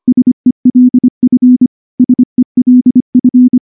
Morse code, heard as a keyed tone, about 24 wpm, 250 Hz. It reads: SELF SELF
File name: morse-training-25WPM